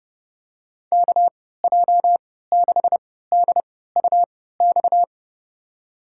Morse code, heard as a keyed tone, about 30 wpm, spaced as for 17 wpm, 700 Hz. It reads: KJ6DUX
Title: USA Call Signs - 6 Characters - 3x Character Spacing - 30wpm